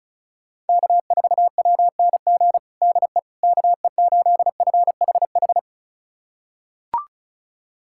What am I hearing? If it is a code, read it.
Morse code, heard as a keyed tone, about 36 wpm, 700 Hz. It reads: K4WNG DE KE8FHH
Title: USA Callsign DE Callsign 35wpm